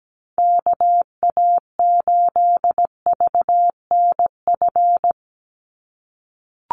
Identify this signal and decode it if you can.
Morse code, heard as a keyed tone, about 17 wpm, 700 Hz. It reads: KA8VNF
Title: USA Call Signs 17wpm